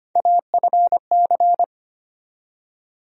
Morse code, heard as a keyed tone, about 25 wpm, 700 Hz. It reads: AFC
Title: USA Call Sign Postfixes - 3 Letters 25wpm